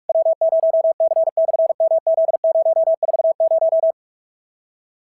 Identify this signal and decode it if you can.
Morse code, heard as a keyed tone, about 44 wpm, 650 Hz. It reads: W0C/MZ040
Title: SOTA QSO Practice 45wpm